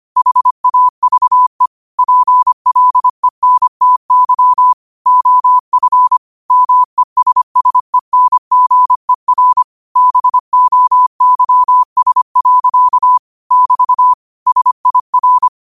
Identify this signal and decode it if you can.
Morse code, heard as a keyed tone, about 25 wpm, 1000 Hz. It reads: HAVE PLENTY OF MESSENGER BOYS. <BT> SIR